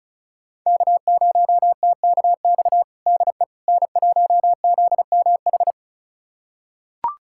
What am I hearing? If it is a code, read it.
Morse code, heard as a keyed tone, about 34 wpm, 700 Hz. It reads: K0TKX DE N1ZMH
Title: USA Callsign DE Callsign - 5 Characters - 35wpm